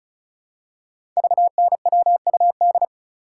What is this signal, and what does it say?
Morse code, heard as a keyed tone, about 36 wpm, 690 Hz. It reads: VNWUD
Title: Code Groups - 5 Letters - 35wpm